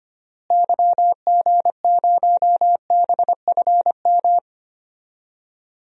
Morse code, heard as a keyed tone, about 25 wpm, 700 Hz. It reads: YG0BFM
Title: DXCC-100 Call Signs - 6 Characters - Repeated 2x - 25wpm